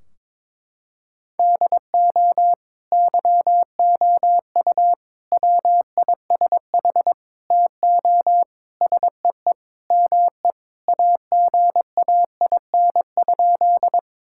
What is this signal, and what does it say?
Morse code, heard as a keyed tone, about 22 wpm, 700 Hz. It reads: DO YOU WISH TO SEE ME AGAIN?
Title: Sentences from Top 1600 Words 22wpm